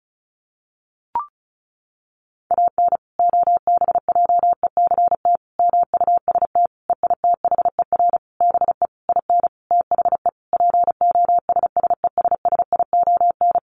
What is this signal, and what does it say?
Morse code, heard as a keyed tone, about 34 wpm, 700 Hz. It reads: AN OBJECT MUST EITHER BE IN THE POSSESSION
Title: Sentences from Top 1400 Words 35wpm